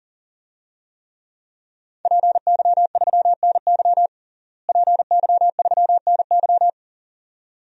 Morse code, heard as a keyed tone, about 40 wpm, 700 Hz. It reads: PY3NY PY3NY
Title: DXCC-100 Call Signs - 5 Characters - Repeated 2x - 40wpm